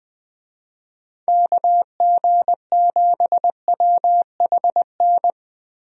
Morse code, heard as a keyed tone, about 21 wpm, 700 Hz. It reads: KG7WHN
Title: USA Call Signs - 6 Characters - Repeated 2x - 20wpm